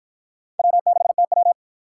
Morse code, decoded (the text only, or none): W6TW